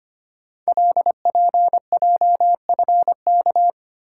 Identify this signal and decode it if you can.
Morse code, heard as a keyed tone, about 25 wpm, 700 Hz. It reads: LPJFK
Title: Code Groups - 5 Letters - 25wpm